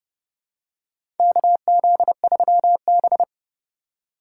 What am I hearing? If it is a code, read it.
Morse code, heard as a keyed tone, about 29 wpm, 700 Hz. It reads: KZ3B
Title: USA Call Signs - 4 Characters - 8x Character Spacing - 30wpm